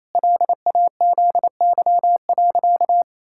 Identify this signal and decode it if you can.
Morse code, heard as a keyed tone, about 27 wpm, 700 Hz. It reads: LAZY.
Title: Pangram Challenge v2 - 28wpm